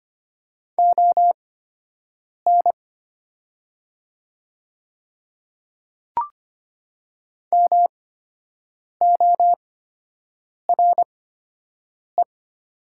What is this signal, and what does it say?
Morse code, heard as a keyed tone, about 25 wpm, spaced as for 7 wpm, 700 Hz. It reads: ON MORE
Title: 8x Character Spacing - Top 100 Words - 25wpm